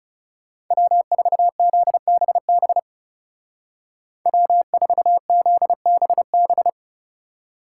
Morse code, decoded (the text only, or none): W4ZBB W4ZBB